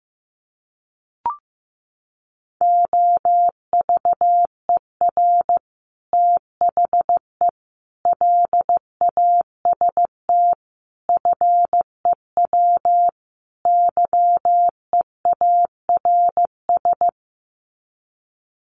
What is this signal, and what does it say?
Morse code, heard as a keyed tone, about 15 wpm, 700 Hz. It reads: OVER THE LAST FEW YEARS
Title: Sets of 5 Words Encore 15wpm